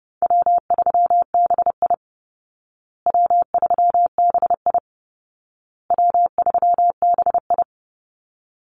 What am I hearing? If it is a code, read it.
Morse code, heard as a keyed tone, about 30 wpm, 700 Hz. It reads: W3BI W3BI W3BI
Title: Call Signs Repeated 2x - 30wpm